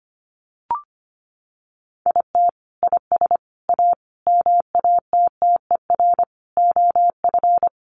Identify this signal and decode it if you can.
Morse code, heard as a keyed tone, about 25 wpm, 700 Hz. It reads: IT IS A MATTER OF
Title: Sentences from Top 1100 Words 25wpm